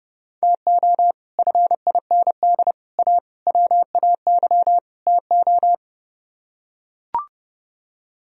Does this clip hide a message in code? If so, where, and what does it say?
Yes: Morse code from 0.4 to 5.7 s, heard as a keyed tone, about 31 wpm, 700 Hz. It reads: TO FIND A WAY TO